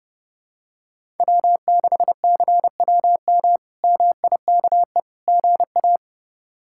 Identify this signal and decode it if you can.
Morse code, heard as a keyed tone, about 30 wpm, 700 Hz. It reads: W6CWM MIKE GA
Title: K1USN SST QSO Practice - 30wpm